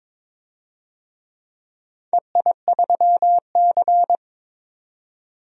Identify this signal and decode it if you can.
Morse code, heard as a keyed tone, about 22 wpm, 700 Hz. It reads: EI3C
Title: DXCC-100 Call Signs - 4 Characters - 22wpm